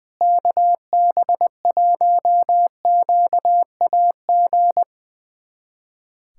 Morse code, heard as a keyed tone, about 20 wpm, 700 Hz. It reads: KB1QAG
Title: USA Callsign DE Callsign 20wpm